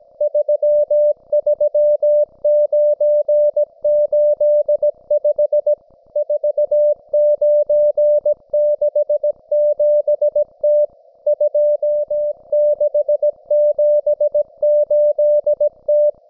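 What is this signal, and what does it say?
Morse code, heard as a keyed tone, about 17 wpm, 590 Hz. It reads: 33985 4967T 2678T
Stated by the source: Russian military Morse station, M32, RDL, on 5268 kHz.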